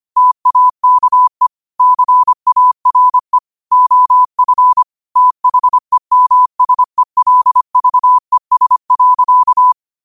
Morse code, heard as a keyed tone, about 25 wpm, 1000 Hz. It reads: TAKE CARE OF THEMSELVES.